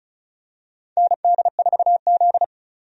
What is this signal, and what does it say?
Morse code, heard as a keyed tone, about 34 wpm, 700 Hz. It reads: ND4Z
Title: USA Call Signs - 4 Characters - 35wpm